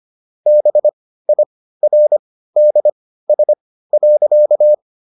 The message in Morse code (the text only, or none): BIRDS.